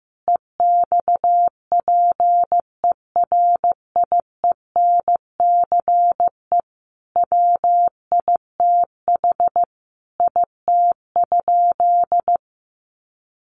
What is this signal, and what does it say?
Morse code, heard as a keyed tone, about 15 wpm, 700 Hz. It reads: EXPERIENCE WITH IT?
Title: Sentences from Top 600 Words 15wpm